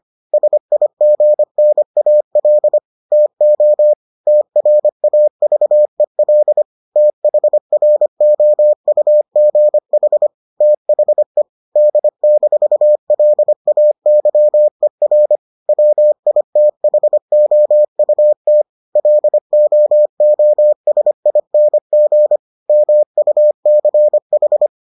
Morse code, heard as a keyed tone, about 25 wpm, 600 Hz. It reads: SIGNAL TO TRAVEL THROUGH THE D-LAYER WITHOUT LOOSING MUCH